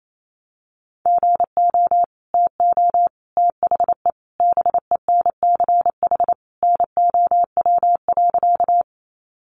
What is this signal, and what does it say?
Morse code, heard as a keyed tone, about 28 wpm, 700 Hz. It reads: GO TO THE BENCH NOW.
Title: Sentences from Top 1400 Words 28wpm